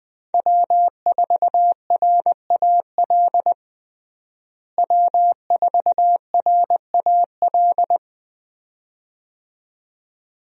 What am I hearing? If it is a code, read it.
Morse code, heard as a keyed tone, about 20 wpm, 700 Hz. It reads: W4RAL W4RAL
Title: USA Call Signs - 5 Characters - Repeated 2x - 20wpm